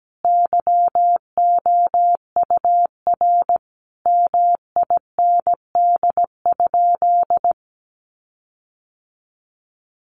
Morse code, heard as a keyed tone, about 17 wpm, 700 Hz. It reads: YOUR MIND?
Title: Sentences from Top 1300 Words 17wpm